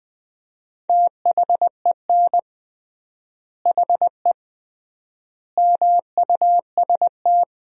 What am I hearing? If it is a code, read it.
Morse code, heard as a keyed tone, about 20 wpm, 700 Hz. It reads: THEN HE MUST
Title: 3x Word-Spacing - Sentences from Top 500 Words - 20wpm